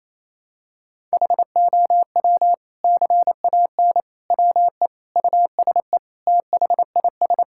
Morse code, heard as a keyed tone, about 27 wpm, 700 Hz. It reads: HOW CAN WE USE THIS
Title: Sentences from Top 1000 Words - Review Part 5 - 28wpm